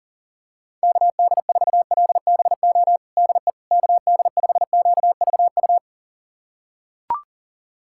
Morse code, heard as a keyed tone, about 40 wpm, 700 Hz. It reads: KD4LBO DE KD5QVU
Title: USA Callsign DE Callsign - 6 Characters - 40wpm